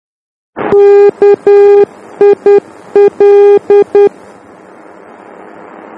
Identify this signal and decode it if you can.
Morse code, heard as a keyed tone, about 10 wpm, 410 Hz. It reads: KIL
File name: germany / mx / KIL-D